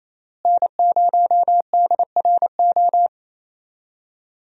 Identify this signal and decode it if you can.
Morse code, heard as a keyed tone, about 28 wpm, 700 Hz. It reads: N0DRO
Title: USA Call Signs - 5 Characters - Repeated 3x - 28wpm